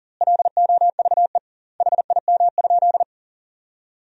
Morse code, heard as a keyed tone, about 40 wpm, 700 Hz. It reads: LOVE HIM?
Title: 2x Word-Spacing - Sentences from Top 400 Words - 40wpm